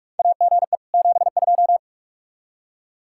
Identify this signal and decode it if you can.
Morse code, heard as a keyed tone, about 44 wpm, 700 Hz. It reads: AGE 72
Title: Practice QSO - Exchange 3 - More Info - 45wpm